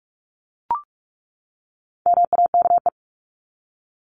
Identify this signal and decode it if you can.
Morse code, heard as a keyed tone, about 44 wpm, 700 Hz. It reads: MAKE
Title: ES:S Suffix - Top 100 Words with Base Words 45wpm